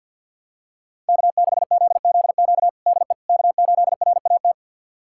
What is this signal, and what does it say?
Morse code, heard as a keyed tone, about 49 wpm, 700 Hz. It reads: K6ZZQ DE K8RAT
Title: USA Callsign DE Callsign 50wpm